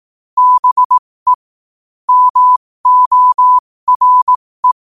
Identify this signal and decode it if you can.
Morse code, heard as a keyed tone, about 18 wpm, spaced as for 15 wpm, 1000 Hz. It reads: BE MORE
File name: 15wpm